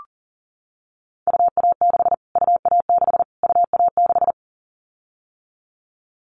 Morse code, heard as a keyed tone, about 40 wpm, 700 Hz. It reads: UA6 UA6 UA6
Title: Callsign Prefixes for Top 25 DXCC - Repeated 3x - 40wpm